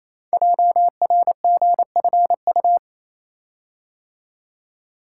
Morse code, heard as a keyed tone, about 28 wpm, 700 Hz. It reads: JRGFU
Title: Code Groups - 5 Letters - 28wpm